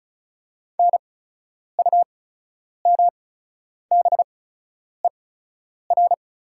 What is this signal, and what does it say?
Morse code, heard as a keyed tone, about 36 wpm, spaced as for 10 wpm, 700 Hz. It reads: NUMBER
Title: 8x Character Spacing - Top 200 Words - 35wpm